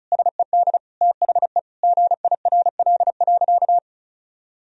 Morse code, heard as a keyed tone, about 36 wpm, 700 Hz. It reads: SED THE GIRL.
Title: Sentences from Top 400 Words and Abbreviations 35wpm